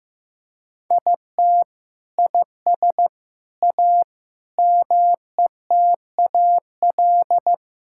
Morse code, heard as a keyed tone, about 15 wpm, 700 Hz. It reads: IT IS A METAL
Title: Sentences from Top 1000 Words - Review Part 6 - 15wpm